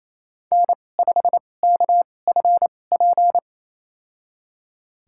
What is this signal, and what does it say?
Morse code, heard as a keyed tone, about 28 wpm, spaced as for 20 wpm, 700 Hz. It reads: N5KFP
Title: USA Call Signs - 2x Character Spacing - 28wpm